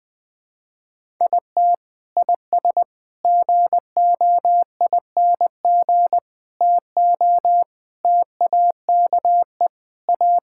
Morse code, heard as a keyed tone, about 20 wpm, 700 Hz. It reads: IT IS GOING TO TAKE A
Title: Sentences with 4-Word Phrases - Spoken Beginning - 20wpm